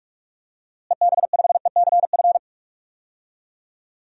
Morse code, heard as a keyed tone, about 42 wpm, 700 Hz. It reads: EB5ECF
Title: DXCC-100 Call Signs - 6 Characters - Repeated 2x - 45wpm